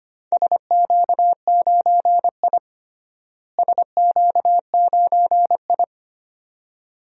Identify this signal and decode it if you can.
Morse code, heard as a keyed tone, about 25 wpm, 700 Hz. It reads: SQ9I SQ9I
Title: DXCC-100 Call Signs - 4 Characters - Repeated 2x - 25wpm